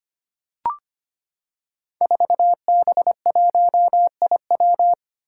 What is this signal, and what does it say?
Morse code, heard as a keyed tone, about 25 wpm, 700 Hz. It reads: 4B1IW
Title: Five Challenging Letter-Number Combinations 25wpm